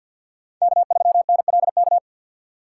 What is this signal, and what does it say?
Morse code, heard as a keyed tone, about 47 wpm, 700 Hz. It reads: K3NLK
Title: USA Call Signs - 6x Character Spacing - 50wpm